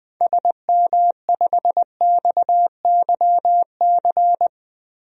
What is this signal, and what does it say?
Morse code, heard as a keyed tone, about 20 wpm, 700 Hz. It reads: SM5XYC
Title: DXCC-100 Call Signs - 6 Characters - 20wpm